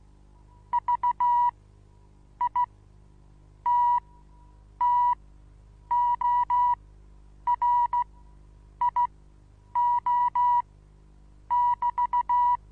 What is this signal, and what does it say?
Morse code, heard as a keyed tone, about 15 wpm, spaced as for 8 wpm, 970 Hz. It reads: VITTORIO<BT>
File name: Onlinelehrgang / morsen / DJ4UF Morsen25